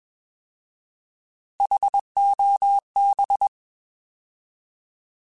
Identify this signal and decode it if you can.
Morse code, heard as a keyed tone, about 21 wpm, 800 Hz. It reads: HOB
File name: Alphabetical / Alphabetical mp3s / Hs @20